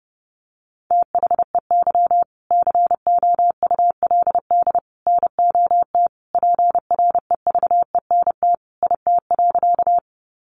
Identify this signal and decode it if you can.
Morse code, heard as a keyed tone, about 30 wpm, 700 Hz. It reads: THEY COULD NOT PREVENT IT.
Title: Sentences from Top 1100 Words 30wpm